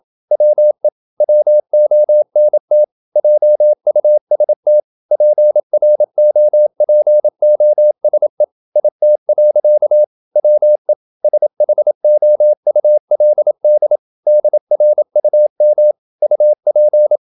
Morse code, heard as a keyed tone, about 27 wpm, 600 Hz. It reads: WE WONT JUST PROPOSE IT. WE SHOULD DRUM UP
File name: Never 0027wpm